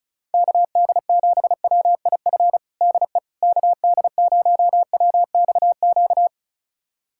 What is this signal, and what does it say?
Morse code, heard as a keyed tone, about 36 wpm, 700 Hz. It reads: KD7WIF DE KD0WXQ
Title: USA Callsign DE Callsign - 6 Characters - 35wpm